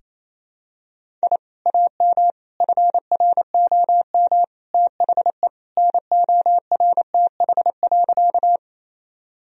Morse code, heard as a keyed tone, about 28 wpm, 700 Hz. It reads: I AM FROM THE NORTH.
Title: Sentences from Top 1000 Words - Review Part 1 - 28wpm